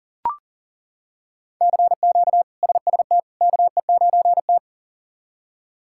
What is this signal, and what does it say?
Morse code, heard as a keyed tone, about 40 wpm, 700 Hz. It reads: CQ SST KE9T
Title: K1USN SST QSO Practice - 40wpm